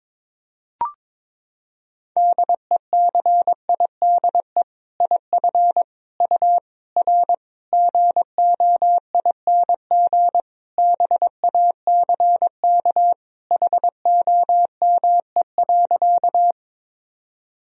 Morse code, heard as a keyed tone, about 22 wpm, 700 Hz. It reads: DECIDE IF U R GOING BACK HOME.
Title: Sentences from Top 500 Words and Abbreviations 22wpm